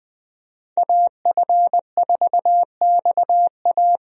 Morse code, heard as a keyed tone, about 20 wpm, 700 Hz. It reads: AF4XA
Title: USA Call Signs - 5 Characters - Repeated 2x - 20wpm